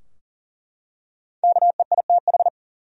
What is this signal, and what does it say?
Morse code, heard as a keyed tone, about 40 wpm, 700 Hz. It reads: KEITH
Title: USA Common Names - Max 5 Letters - 40wpm